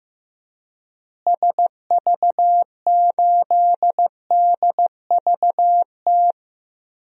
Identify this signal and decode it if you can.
Morse code, heard as a keyed tone, about 15 wpm, 700 Hz. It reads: SV8DVT
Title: DXCC-100 Call Signs - 6 Characters - Repeated 3x - 15wpm